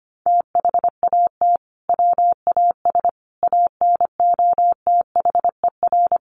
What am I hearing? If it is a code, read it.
Morse code, heard as a keyed tone, about 25 wpm, 700 Hz. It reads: THAT WAS ANOTHER
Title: Sentences from Top 600 Words 25wpm